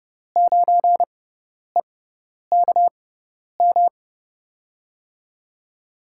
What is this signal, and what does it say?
Morse code, heard as a keyed tone, about 32 wpm, spaced as for 10 wpm, 700 Hz. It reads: 9EKM
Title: USA Call Signs - 6 Characters - 6x Character Spacing - 30wpm